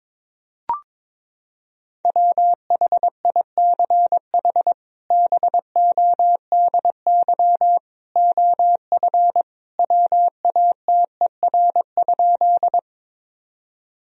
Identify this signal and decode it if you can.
Morse code, heard as a keyed tone, about 22 wpm, 700 Hz. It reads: WHICH BODY OF WATER?
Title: Sentences from Top 1000 Words - Review Part 2 - 22wpm